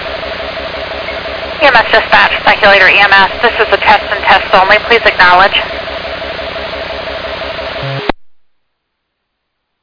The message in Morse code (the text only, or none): none